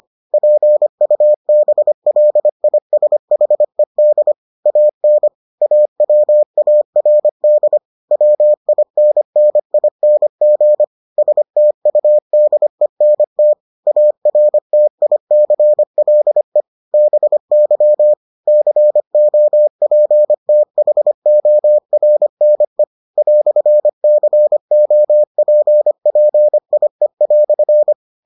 Morse code, heard as a keyed tone, about 25 wpm, 600 Hz. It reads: PUBLISHED AN AWARD WINNING STUDENT ARTICLE BY COPTHORNE "COPPIE"